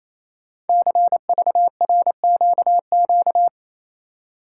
Morse code, heard as a keyed tone, about 27 wpm, 700 Hz. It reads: CVRQQ
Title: Code Groups - 5 Letters - 28wpm